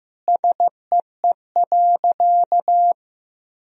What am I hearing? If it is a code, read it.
Morse code, heard as a keyed tone, about 15 wpm, 700 Hz. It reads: SEE.
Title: Sentences from Top 100 Words and Abbreviations 15wpm